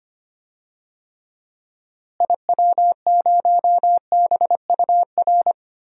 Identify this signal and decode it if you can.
Morse code, heard as a keyed tone, about 25 wpm, 700 Hz. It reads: IW0BUR
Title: DXCC-100 Call Signs - 6 Characters - Repeated 3x - 25wpm